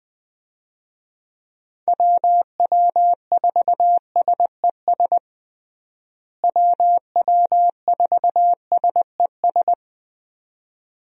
Morse code, heard as a keyed tone, about 20 wpm, 700 Hz. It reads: WW4SES WW4SES
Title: USA Call Signs - 6 Characters - Repeated 2x - 20wpm